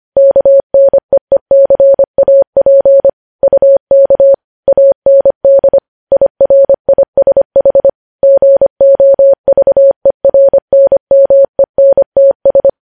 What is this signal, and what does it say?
Morse code, heard as a keyed tone, about 25 wpm, 550 Hz. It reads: KNEECAP UK AND IRISH GOVERNMENTS
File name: podcast / News headlines-1x-25x25